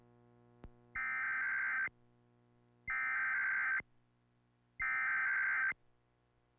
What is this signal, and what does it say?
Emergency Alert System (SAME) tones (header bursts), heard as harsh buzzing bursts. It is heard from 1.0 to 5.7 s.